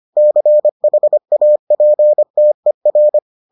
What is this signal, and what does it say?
Morse code, heard as a keyed tone, about 25 wpm, 600 Hz. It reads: CHAPTER